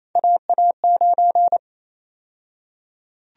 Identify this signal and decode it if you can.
Morse code, heard as a keyed tone, about 28 wpm, 700 Hz. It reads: AA9
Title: USA Call Sign Prefixes - 28wpm